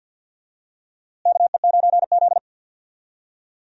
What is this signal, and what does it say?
Morse code, heard as a keyed tone, about 52 wpm, 700 Hz. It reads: KE9Z
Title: USA Call Signs - 4 Characters - Repeated 2x - 50wpm